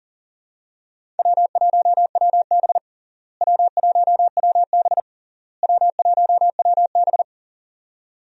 Morse code, heard as a keyed tone, about 40 wpm, 700 Hz. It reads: W1WB W1WB W1WB
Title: USA Call Signs - 4 Characters - Repeated 3x - 40wpm